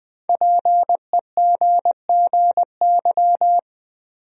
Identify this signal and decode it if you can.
Morse code, heard as a keyed tone, about 20 wpm, 700 Hz. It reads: PEGGY
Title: USA Common Names - Max 5 Letters - 20wpm